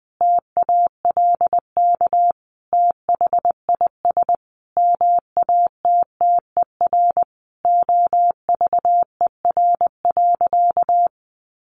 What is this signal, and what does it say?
Morse code, heard as a keyed tone, about 20 wpm, 700 Hz. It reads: TALK THIS MATTER OVER.
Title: Sentences from Top 700 Words 20wpm